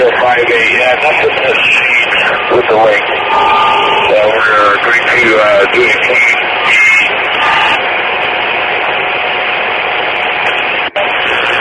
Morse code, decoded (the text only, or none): none